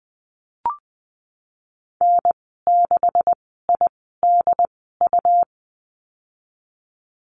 Morse code, heard as a keyed tone, about 20 wpm, spaced as for 15 wpm, 700 Hz. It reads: N6IDU
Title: 167 Lesson 27g - Number 7 - Call Signs 20wpm